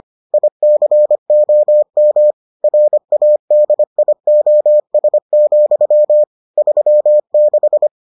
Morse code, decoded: ICOM RADIOS, 36

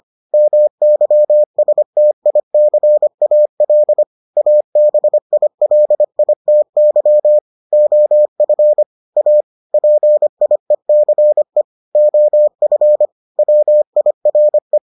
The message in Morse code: MYSTICAL ABILITY OF A PIECE OF WIRE

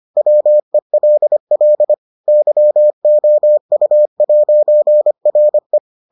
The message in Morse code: WELL YOU'RE

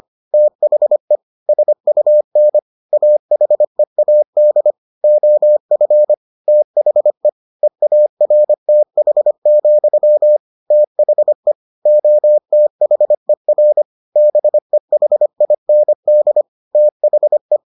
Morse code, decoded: THE SUN AHEAD OF THE EARTH, THE OTHER BEHIND THE

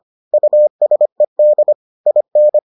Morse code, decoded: USED IN

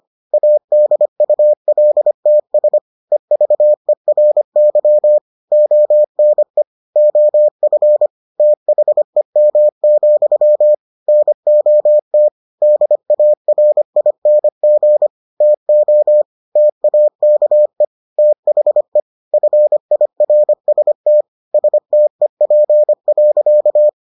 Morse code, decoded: ADULTS EVERY ONE OF THEM, NOT DARING TO TAKE THE FIRST STEP.

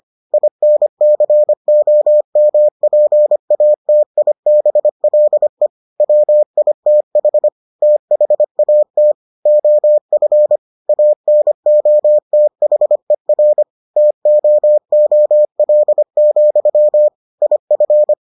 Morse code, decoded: INCOMPATIBLE WITH THAT OF ANOTHER TOOL, IF

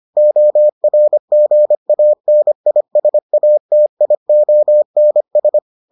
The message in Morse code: ORGANISATIONS

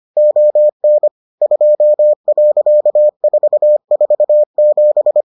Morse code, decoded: ON 2.447

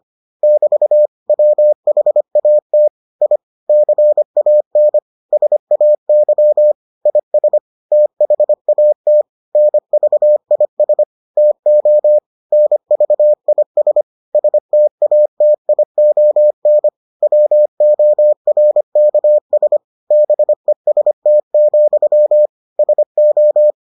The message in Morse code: <BT> WHAT I CAN SAY IS THAT NVIS TO NVIS STATION WORKS BEST, SO

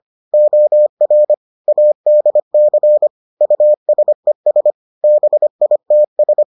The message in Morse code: OR ADC USES BITS

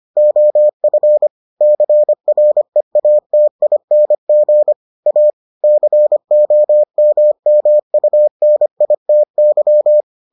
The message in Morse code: OF CREATING A COMMUNITY

